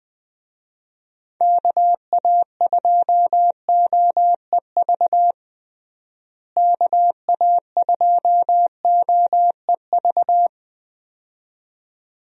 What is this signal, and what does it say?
Morse code, heard as a keyed tone, about 21 wpm, 700 Hz. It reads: KA2OEV KA2OEV
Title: USA Call Signs - 6 Characters - Repeated 2x - 20wpm